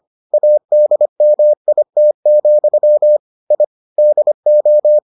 Morse code, heard as a keyed tone, about 25 wpm, 600 Hz. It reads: ADMIT, I DO